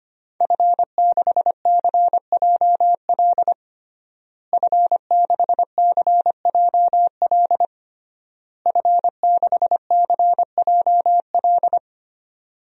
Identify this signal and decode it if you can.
Morse code, heard as a keyed tone, about 25 wpm, 700 Hz. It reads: F6CJL F6CJL F6CJL
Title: DXCC-100 Call Signs - 5 Characters - Repeated 3x - 25wpm